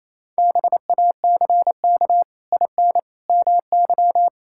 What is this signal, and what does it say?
Morse code, heard as a keyed tone, about 27 wpm, 700 Hz. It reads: BACK IN MY
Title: Pangram Challenge v2 - 28wpm